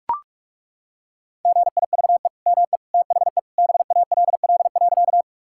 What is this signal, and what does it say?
Morse code, heard as a keyed tone, about 42 wpm, 700 Hz. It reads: GIVE ME THE BALL.
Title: Sentences from Top 1000 Words - Review Part 6 - 45wpm